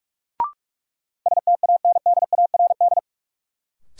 Morse code, heard as a keyed tone, about 46 wpm, 700 Hz. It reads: STANDARD
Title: Sentences from Top 1000 Words - Review Part 6 - New Word Drill - 45wpm